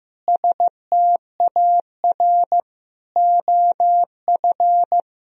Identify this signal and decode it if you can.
Morse code, heard as a keyed tone, about 15 wpm, 700 Hz. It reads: STAR OF
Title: Sentences from Top 1000 Words - Review Part 2 - 15wpm